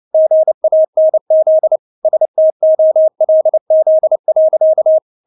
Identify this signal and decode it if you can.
Morse code, heard as a keyed tone, about 29 wpm, 630 Hz. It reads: GANZ STOLZ.
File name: Witz10 29wpm